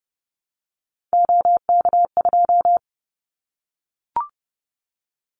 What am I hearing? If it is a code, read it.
Morse code, heard as a keyed tone, about 30 wpm, 700 Hz. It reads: OK2
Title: Callsign Prefixes for Top 25 DXCC - 30wpm